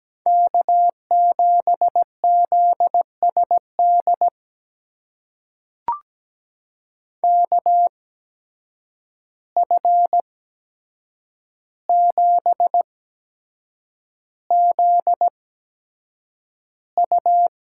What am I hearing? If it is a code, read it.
Morse code, heard as a keyed tone, about 17 wpm, spaced as for 7 wpm, 700 Hz. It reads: K7ZSD KF7ZU
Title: USA Call Signs - 5 Characters - 8x Character Spacing - 17wpm